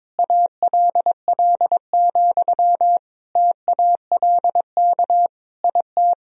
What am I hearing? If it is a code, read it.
Morse code, heard as a keyed tone, about 22 wpm, 700 Hz. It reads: ALL, TALK IT
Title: Sentences from Top 1000 Words - Review Part 2 - 22wpm